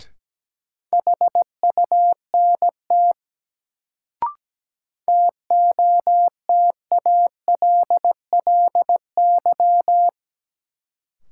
Morse code, heard as a keyed tone, about 17 wpm, 700 Hz. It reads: HUNT TOTALLY
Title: Sentences from Top 1600 Words - New Word Drill Alt - 17wpm